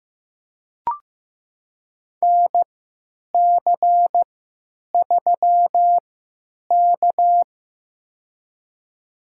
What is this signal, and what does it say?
Morse code, heard as a keyed tone, about 16 wpm, spaced as for 9 wpm, 700 Hz. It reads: NC3K
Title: USA Call Signs - 4 Characters - 3x Character Spacing - 15wpm